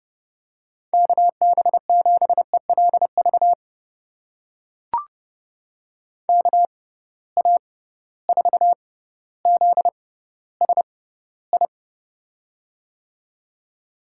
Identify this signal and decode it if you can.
Morse code, heard as a keyed tone, about 30 wpm, spaced as for 15 wpm, 700 Hz. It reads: KB7ELV KA4ZSI
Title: USA Call Signs - 6 Characters - 6x Character Spacing - 30wpm